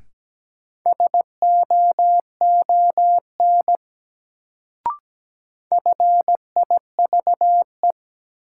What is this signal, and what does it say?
Morse code, heard as a keyed tone, about 17 wpm, 700 Hz. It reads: SOON FIVE
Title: Sentences from Top 400 Words - New Word Drill - 17wpm